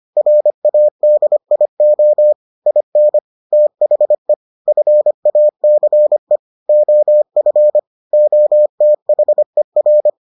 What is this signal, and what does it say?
Morse code, heard as a keyed tone, about 25 wpm, 600 Hz. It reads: RADIO IN THE FACE OF OTHER